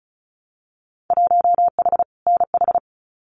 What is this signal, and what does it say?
Morse code, heard as a keyed tone, about 36 wpm, 700 Hz. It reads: 1H NH
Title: Winter Field Day Exchanges 35wpm